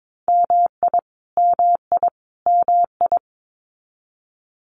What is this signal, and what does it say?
Morse code, heard as a keyed tone, about 22 wpm, 700 Hz. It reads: MI MI MI
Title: US States Repeated 3x 22wpm